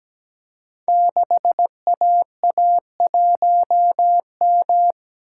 Morse code, heard as a keyed tone, about 17 wpm, 700 Hz. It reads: 6AA1M
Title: Code Groups - 5 Letter:Numbers - 17wpm